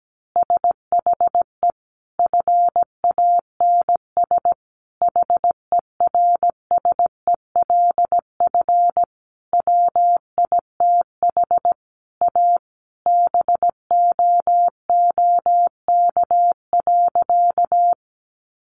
Morse code, heard as a keyed tone, about 17 wpm, 700 Hz. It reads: SHE FANS HERSELF WITH A BOOK.
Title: Additional Easy Phrases 17wpm